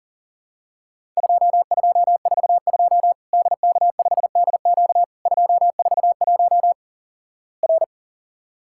Morse code, heard as a keyed tone, about 43 wpm, 700 Hz. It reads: 2242 DK5DQ 241 R
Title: Worked All Europe DX Contest - QTC Exchange - 40wpm